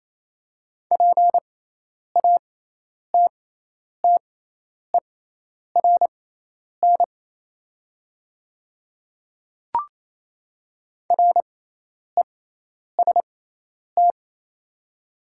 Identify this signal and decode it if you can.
Morse code, heard as a keyed tone, about 27 wpm, spaced as for 10 wpm, 700 Hz. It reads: PATTERN REST
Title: 6x Character Spacing - Top 500 Words - 28wpm